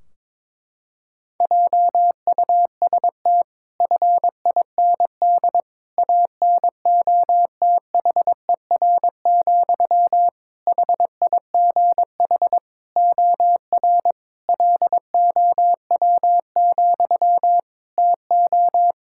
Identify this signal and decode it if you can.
Morse code, heard as a keyed tone, about 22 wpm, 700 Hz. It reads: JUST FIND ANOTHER, HIGH OR LOW, TO
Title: Sentences from Top 1600 Words 22wpm